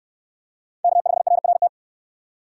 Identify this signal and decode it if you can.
Morse code, heard as a keyed tone, about 64 wpm, 700 Hz. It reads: K5RRT